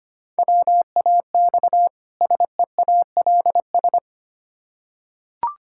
Morse code, heard as a keyed tone, about 25 wpm, 700 Hz. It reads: WAX SEALS
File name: practice / 25wpm / Difficult Words with Hints - 25wpm